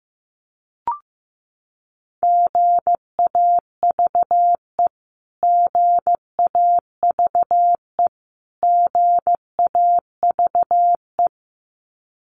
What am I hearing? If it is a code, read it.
Morse code, heard as a keyed tone, about 15 wpm, 700 Hz. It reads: GAVE GAVE GAVE
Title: Top 500 Words Repeated 15wpm